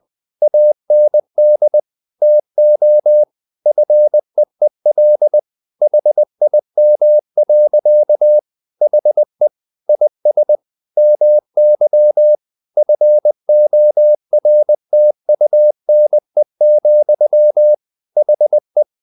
Morse code, heard as a keyed tone, about 20 wpm, 600 Hz. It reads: AND TO FEEL HIM. HE IS MY FORTUNE, HE
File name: the old man and the sea part 2 20wpm